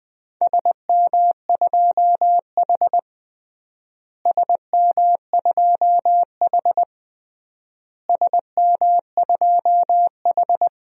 Morse code, heard as a keyed tone, about 20 wpm, 700 Hz. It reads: SM2H SM2H SM2H
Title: DXCC-100 Call Signs - 4 Characters - Repeated 3x - 20wpm